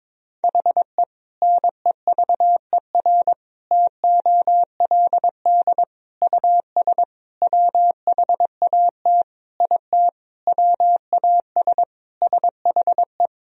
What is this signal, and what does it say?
Morse code, heard as a keyed tone, about 22 wpm, 700 Hz. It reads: HE NEVER TOLD US WHAT IT WAS SHE
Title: Sentences from Top 400 Words and Abbreviations 22wpm